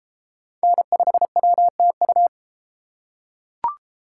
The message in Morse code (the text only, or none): N5WTU